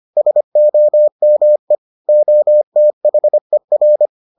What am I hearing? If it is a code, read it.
Morse code, heard as a keyed tone, about 25 wpm, 600 Hz. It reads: SOME OTHER